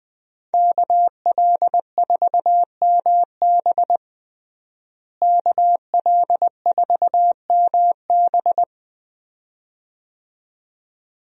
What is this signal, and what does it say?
Morse code, heard as a keyed tone, about 20 wpm, 700 Hz. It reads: KL4MB KL4MB
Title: USA Call Signs - 5 Characters - Repeated 2x - 20wpm